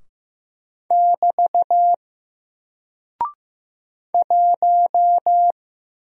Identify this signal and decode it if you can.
Morse code, heard as a keyed tone, about 15 wpm, 700 Hz. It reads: <BT> 1
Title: Single Character (Letter:Number:Punctuation:Prosign) Alt 15wpm